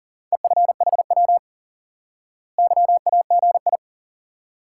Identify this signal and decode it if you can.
Morse code, heard as a keyed tone, about 37 wpm, 700 Hz. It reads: EFHW YAGI
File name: IQE Course 3 - Lesson 89 - YAGI - Two Elements - 40wpm